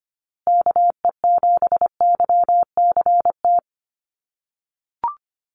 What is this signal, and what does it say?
Morse code, heard as a keyed tone, about 25 wpm, 700 Hz. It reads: KE7YCT
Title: USA Call Signs Repeated 2x 25wpm